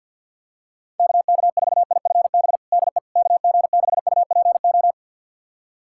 Morse code, heard as a keyed tone, about 49 wpm, 700 Hz. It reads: KK4IFB DE KG6UPO
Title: USA Callsign DE Callsign - 6 Characters - 50wpm